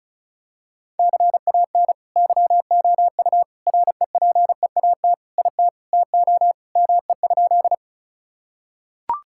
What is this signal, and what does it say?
Morse code, heard as a keyed tone, about 36 wpm, 700 Hz. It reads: CAN YOU REPEAT IT TO ME?
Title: Sentences from Top 1600 Words 35wpm